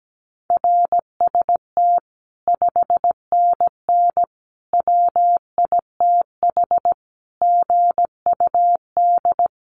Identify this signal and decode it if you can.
Morse code, heard as a keyed tone, about 17 wpm, 700 Hz. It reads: RST 5NN WITH GUD
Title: Common QSO Phrases 17wpm